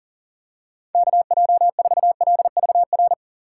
Morse code, heard as a keyed tone, about 40 wpm, 700 Hz. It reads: KJ4LVR
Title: USA Call Signs - 6 Characters - 3x Character Spacing - 40wpm